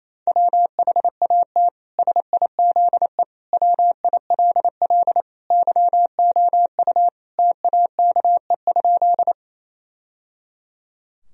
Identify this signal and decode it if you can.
Morse code, heard as a keyed tone, about 28 wpm, 700 Hz. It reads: WHAT SIZE WILL YOU TAKE?
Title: Sentences from Top 1000 Words - Review Part 2 - 28wpm